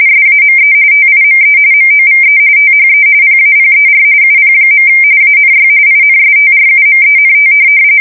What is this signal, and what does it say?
RTTY, about 45.45 baud. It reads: SAME TIME MOVE THE PAPER UP. ANY INSTRUMENT
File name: fsk170 / printing-telegraph